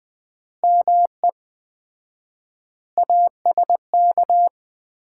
Morse code, heard as a keyed tone, about 20 wpm, 700 Hz. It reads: ME ASK
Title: 4x Word-Spacing - Sets of 5 Words - 20wpm